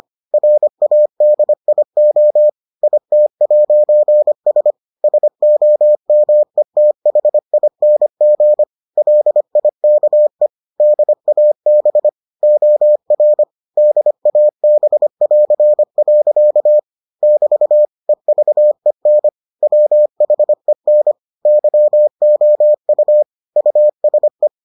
Morse code, heard as a keyed tone, about 25 wpm, 600 Hz. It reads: RADIO IT'S SOMETHING LIKE DAB OR DAB<AR>. <BT> EVEN WHEN YOU USE